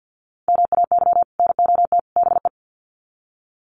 Morse code, heard as a keyed tone, about 50 wpm, 700 Hz. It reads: MAY NOT BE
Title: Sets of 3 Words 50wpm